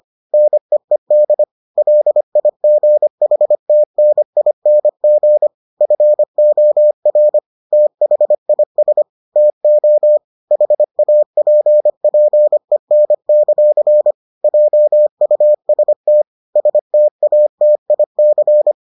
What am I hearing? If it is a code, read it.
Morse code, heard as a keyed tone, about 25 wpm, 600 Hz. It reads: NEED LIGHTNING FOR THIS TO HAPPEN; JUST STATIC